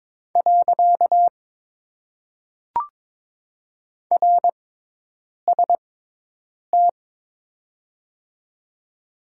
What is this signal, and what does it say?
Morse code, heard as a keyed tone, about 22 wpm, spaced as for 8 wpm, 700 Hz. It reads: . RST
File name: Top 90 QSO Elements - 6x Character Spacing - 22wpm